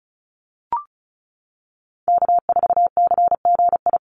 Morse code, heard as a keyed tone, about 34 wpm, 700 Hz. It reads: K4CGI
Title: USA Call Signs 35wpm